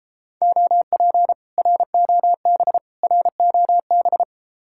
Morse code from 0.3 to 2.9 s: OP ROB

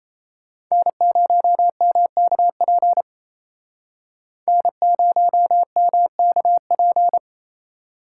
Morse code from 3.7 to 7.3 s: N0MKP

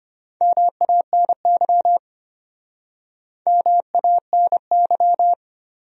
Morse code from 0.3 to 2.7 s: MANY

